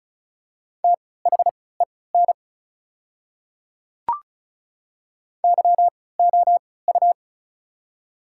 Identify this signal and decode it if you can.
Morse code, heard as a keyed tone, about 36 wpm, spaced as for 20 wpm, 690 Hz. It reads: THEN YOU
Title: 3x Character Spacing - Top 100 Words - 35wpm